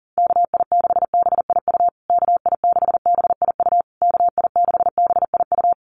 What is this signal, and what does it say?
Morse code, heard as a keyed tone, about 40 wpm, 700 Hz. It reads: KI6BIU KI6BIU KI6BIU
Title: USA Call Signs Repeated 3x 40wpm